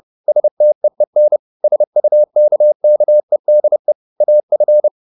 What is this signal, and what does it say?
Morse code, heard as a keyed tone, about 30 wpm, 600 Hz. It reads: STEEN SUKKEDE AF
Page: Ferie 30 wpm